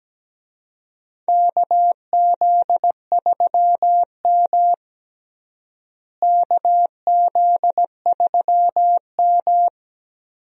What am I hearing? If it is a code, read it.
Morse code, heard as a keyed tone, about 17 wpm, 700 Hz. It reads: KZ3M KZ3M
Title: USA Call Signs - 4 Characters - Repeated 2x - 17wpm